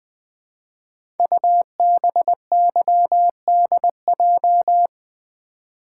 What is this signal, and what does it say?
Morse code, heard as a keyed tone, about 20 wpm, 700 Hz. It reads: UBYDJ
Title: Code Groups - 5 Letters - 20wpm